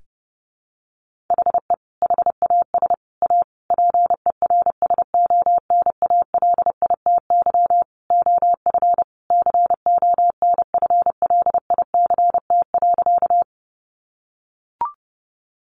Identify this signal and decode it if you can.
Morse code, heard as a keyed tone, about 30 wpm, 700 Hz. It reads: HE HAS A PERSONALITY OF CONFLICT.
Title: Sentences from Top 1400 Words 30wpm